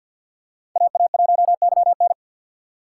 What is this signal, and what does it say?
Morse code, heard as a keyed tone, about 47 wpm, 700 Hz. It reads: AA1YN
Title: USA Call Signs - 6x Character Spacing - 50wpm